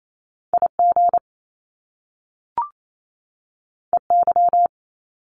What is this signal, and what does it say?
Morse code, heard as a keyed tone, about 28 wpm, 700 Hz. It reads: IG EY
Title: Top Two Letter Combinations in Words 28wpm